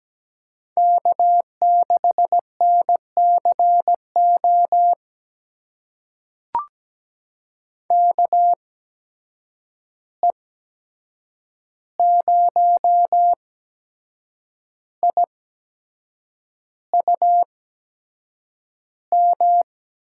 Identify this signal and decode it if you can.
Morse code, heard as a keyed tone, about 17 wpm, spaced as for 7 wpm, 700 Hz. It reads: K6NCO KE0IUM
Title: USA Call Signs - 8x Character Spacing - 17wpm